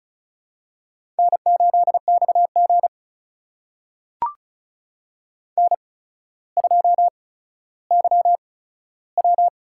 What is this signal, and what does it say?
Morse code, heard as a keyed tone, about 36 wpm, spaced as for 15 wpm, 700 Hz. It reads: N8XG N2YW
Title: USA Call Signs - 4 Characters - 8x Character Spacing - 35wpm